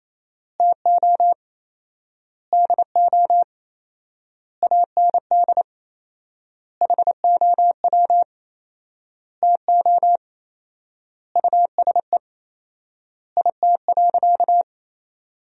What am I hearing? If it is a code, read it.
Morse code, heard as a keyed tone, about 28 wpm, 700 Hz. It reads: TO DO AND HOW TO USE IT.
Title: 4x Word-Spacing - Sentences from Top 500 Words - 28wpm